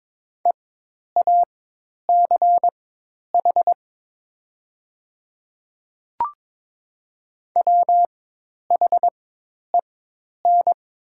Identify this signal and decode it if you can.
Morse code, heard as a keyed tone, about 22 wpm, spaced as for 10 wpm, 700 Hz. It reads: EACH WHEN
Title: 4x Character Spacing - Top 100 Words - 22wpm